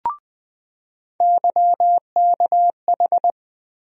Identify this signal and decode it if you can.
Morse code, heard as a keyed tone, about 20 wpm, 700 Hz. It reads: YKH
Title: USA Call Sign Postfixes - 3 Letters 20wpm